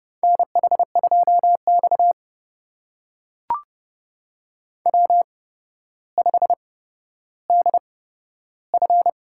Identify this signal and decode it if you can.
Morse code, heard as a keyed tone, about 30 wpm, spaced as for 13 wpm, 700 Hz. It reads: NH2X W5DF
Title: USA Call Signs - 4 Characters - 8x Character Spacing - 30wpm